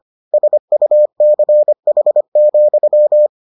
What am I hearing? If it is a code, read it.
Morse code, heard as a keyed tone, about 25 wpm, 600 Hz. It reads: SUCH,